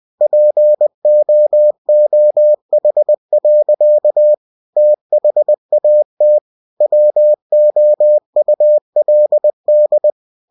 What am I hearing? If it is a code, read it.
Morse code, heard as a keyed tone, about 20 wpm, 600 Hz. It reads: POOH. THAT WOULD
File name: cw-books / winnie-the-pooh / winnie-20wpm-page-0051